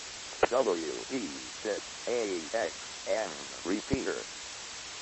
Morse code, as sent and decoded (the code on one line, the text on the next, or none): none